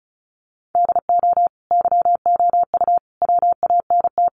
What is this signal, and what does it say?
Morse code, heard as a keyed tone, about 34 wpm, 700 Hz. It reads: DO YOU WANT
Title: Sentences from Top 1500 Words 35wpm